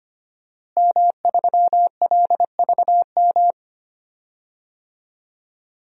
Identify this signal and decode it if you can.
Morse code, heard as a keyed tone, about 25 wpm, 700 Hz. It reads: M3LVM
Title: DXCC-100 Call Signs - 5 Characters - Repeated 2x - 25wpm